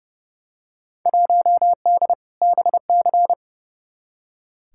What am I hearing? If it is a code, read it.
Morse code, heard as a keyed tone, about 32 wpm, 700 Hz. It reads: 1D BC
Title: Field Day Exchanges - 30wpm